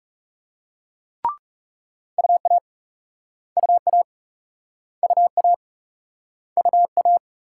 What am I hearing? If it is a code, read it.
Morse code, heard as a keyed tone, about 38 wpm, 700 Hz. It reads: UA UA UA UA